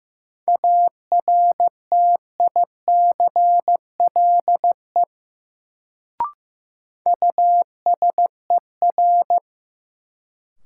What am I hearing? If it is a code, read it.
Morse code, heard as a keyed tone, about 15 wpm, 700 Hz. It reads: ARTICLE USER
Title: Sentences from Top 1000 Words - Review Part 5 - New Word Drill - 15wpm